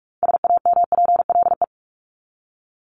Morse code, heard as a keyed tone, about 44 wpm, 700 Hz. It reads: SAMPLE
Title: Top 1501-1750 Words 45wpm